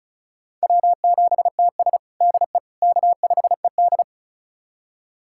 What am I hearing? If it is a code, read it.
Morse code, heard as a keyed tone, about 34 wpm, 700 Hz. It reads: W7TS DE K5ED
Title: USA Callsign DE Callsign - 4 Characters - 35wpm